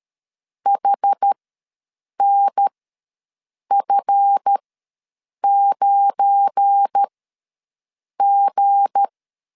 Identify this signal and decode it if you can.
Morse code, heard as a keyed tone, about 13 wpm, spaced as for 7 wpm, 800 Hz. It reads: HNF9G
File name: LEZIONI / 7 GF9Z / 7 GF9 ws 12wpm 3344